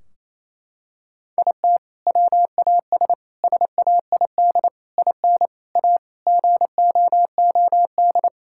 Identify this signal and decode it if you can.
Morse code, heard as a keyed tone, about 28 wpm, 700 Hz. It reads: IT WAS SAID IN A GOOD
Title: Sentences from Top 1600 Words 28wpm